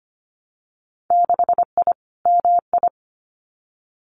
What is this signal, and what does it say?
Morse code, heard as a keyed tone, about 25 wpm, 700 Hz. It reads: 6I MI
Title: Winter Field Day Exchanges Encore 25wpm